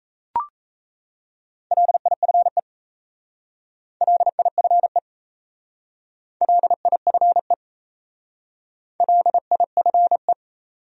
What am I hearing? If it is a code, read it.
Morse code, heard as a keyed tone, about 34 wpm, 700 Hz. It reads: LIFE LIFE LIFE LIFE
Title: Speed-Racing Top 300 Words 28wpm